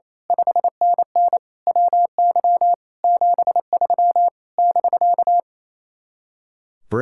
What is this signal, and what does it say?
Morse code, heard as a keyed tone, about 27 wpm, 700 Hz. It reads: 5NN WY 73 <BK>
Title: Speed-Racing Common QSO Phrases - 28wpm